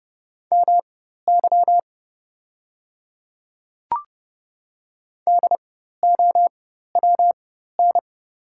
Morse code, heard as a keyed tone, about 30 wpm, spaced as for 14 wpm, 700 Hz. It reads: MY DOWN
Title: 4x Character Spacing - Top 200 Words - 30wpm